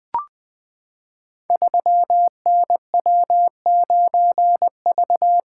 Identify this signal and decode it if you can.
Morse code, heard as a keyed tone, about 20 wpm, 700 Hz. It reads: 3NW9V
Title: Code Groups - 5 Letter:Numbers - 20wpm